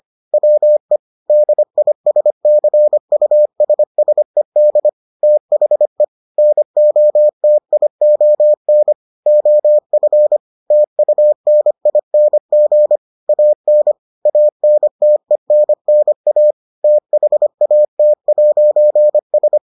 Morse code, heard as a keyed tone, about 25 wpm, 600 Hz. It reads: WE DISCUSSED THE NOTION OF TUNING AN ANTENNA THAT'S